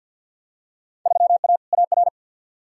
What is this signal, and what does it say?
Morse code, heard as a keyed tone, about 53 wpm, 700 Hz. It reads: 3A AR
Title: Field Day Exchanges - 50wpm